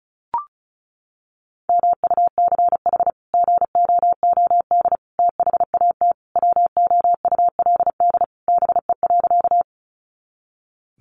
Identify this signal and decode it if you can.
Morse code, heard as a keyed tone, about 34 wpm, 700 Hz. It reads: MUCH GOOD THAT WOULD BE.
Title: Sentences from Top 200 Words 35wpm